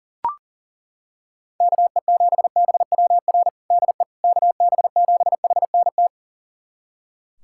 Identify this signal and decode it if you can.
Morse code, heard as a keyed tone, about 40 wpm, 700 Hz. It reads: KE7BWR DE KB7HNT
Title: USA Callsign DE Callsign - 6 Characters - 40wpm